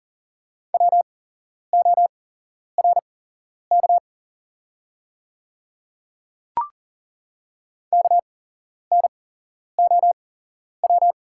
Morse code, heard as a keyed tone, about 40 wpm, spaced as for 11 wpm, 700 Hz. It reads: WORK KNOW
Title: 8x Character Spacing - Top 300 Words - 40wpm